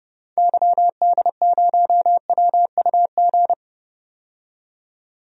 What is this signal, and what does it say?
Morse code, heard as a keyed tone, about 30 wpm, 700 Hz. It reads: YD0WUG
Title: DXCC-100 Call Signs - 6 Characters - 30wpm